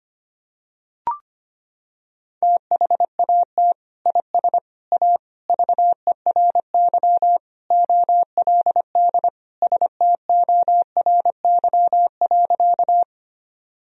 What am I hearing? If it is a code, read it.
Morse code, heard as a keyed tone, about 25 wpm, 700 Hz. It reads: THAT IS A VERY OLD STORY.
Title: Sentences from Top 1000 Words - Review Part 1 - 25wpm